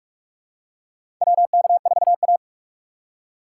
Morse code, heard as a keyed tone, about 46 wpm, 700 Hz. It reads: WK4A
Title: USA Call Signs - 4 Characters - 3x Character Spacing - 45wpm